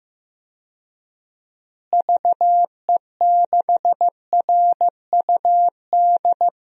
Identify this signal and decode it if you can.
Morse code, heard as a keyed tone, about 15 wpm, 700 Hz. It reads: VE6RUD
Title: DXCC-100 Call Signs - 6 Characters - Repeated 3x - 15wpm